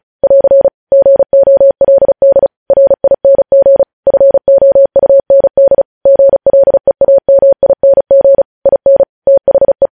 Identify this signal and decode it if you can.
Morse code, heard as a keyed tone, about 36 wpm, 550 Hz. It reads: <AR> GOLD RING FOUND GLEAMING IN THE
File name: podcast / News headlines-1x-35x35